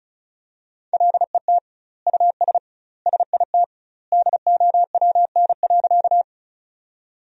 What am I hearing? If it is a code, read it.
Morse code, heard as a keyed tone, about 34 wpm, 700 Hz. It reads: LET US SIT DOWN.
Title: 2x Word-Spacing - Sentences from Top 500 Words - 35wpm